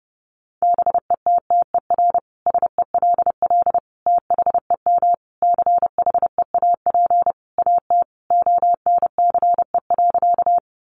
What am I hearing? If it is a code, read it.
Morse code, heard as a keyed tone, about 30 wpm, 700 Hz. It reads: BETTER SELL THEM CHEAP AT ONCE.
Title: Sentences from Top 1200 Words 30wpm